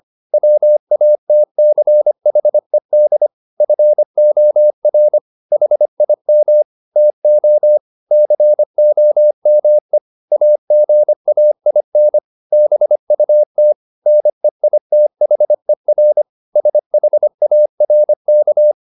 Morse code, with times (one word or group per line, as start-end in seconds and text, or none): 0.3-3.3 WATCHED
3.6-5.2 FOR
5.5-6.6 HIM
7.0-7.8 TO
8.1-10.0 COME
10.3-12.2 AGAIN
12.5-13.7 BUT
14.1-16.2 NEITHER
16.6-18.7 SHARK